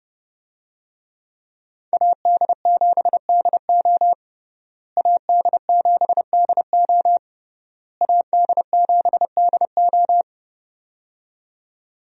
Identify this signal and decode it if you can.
Morse code, heard as a keyed tone, about 30 wpm, 700 Hz. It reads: AD7DO AD7DO AD7DO
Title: USA Call Signs - 5 Characters - Repeated 3x - 30wpm